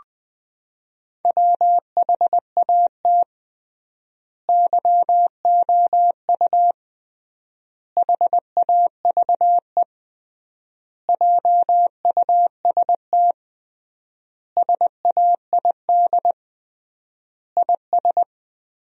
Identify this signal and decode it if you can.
Morse code, heard as a keyed tone, about 21 wpm, 700 Hz. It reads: WHAT YOU HAVE JUST SAID IS
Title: 3x Word-Spacing - Sentences from Top 200 Words - 20wpm